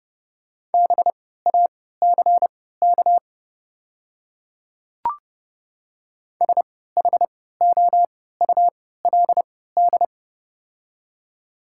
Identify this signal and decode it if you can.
Morse code, heard as a keyed tone, about 31 wpm, spaced as for 17 wpm, 700 Hz. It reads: BACK SHOULD
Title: 3x Character Spacing - Top 500 Words - 30wpm